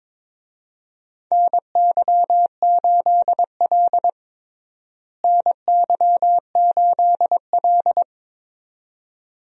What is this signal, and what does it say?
Morse code, heard as a keyed tone, about 22 wpm, 700 Hz. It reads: NY8L NY8L
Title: USA Call Signs - 4 Characters - Repeated 2x - 22wpm